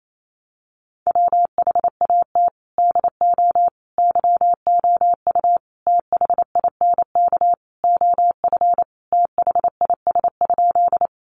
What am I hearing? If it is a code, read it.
Morse code, heard as a keyed tone, about 28 wpm, 700 Hz. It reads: WHAT DO YOU THINK OF THIS?
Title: Sentences from Top 200 Words 28wpm